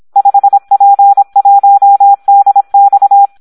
Morse code, heard as a keyed tone, about 26 wpm, 800 Hz. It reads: 5P1DX
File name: Audio practica CW / SD 1 26